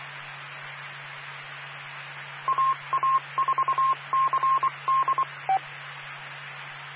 Morse code, heard as a keyed tone, about 24 wpm, 1060 Hz, with radio noise. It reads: AA4CD
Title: Claremont, CA Police, Amateur Radio